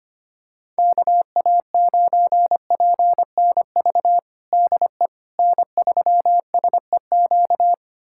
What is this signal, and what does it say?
Morse code, heard as a keyed tone, about 25 wpm, 700 Hz. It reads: KA9PNV DE N3SEQ
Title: USA Callsign DE Callsign 25wpm